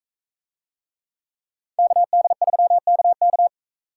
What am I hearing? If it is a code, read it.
Morse code, heard as a keyed tone, about 44 wpm, 700 Hz. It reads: KD3KK